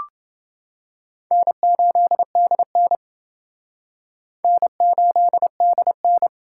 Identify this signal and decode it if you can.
Morse code, heard as a keyed tone, about 28 wpm, 700 Hz. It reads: N8DN N8DN